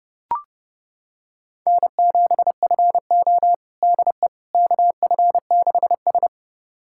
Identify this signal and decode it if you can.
Morse code, heard as a keyed tone, about 32 wpm, 700 Hz. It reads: N7FO DE KF6S
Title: USA Callsign DE Callsign - 4 Characters - 30wpm